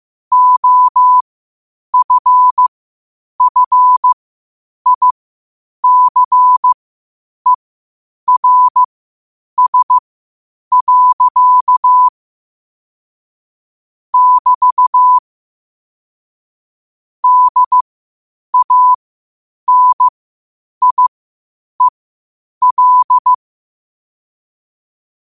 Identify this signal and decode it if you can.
Morse code, heard as a keyed tone, about 15 wpm, spaced as for 8 wpm, 1000 Hz. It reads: OFFICERS. <BT> DANIEL
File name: qotdfiles / 8wpm